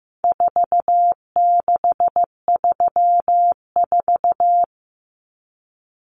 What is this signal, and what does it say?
Morse code, heard as a keyed tone, about 15 wpm, 700 Hz. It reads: 4634
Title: Numbers - 4-Digit 15wpm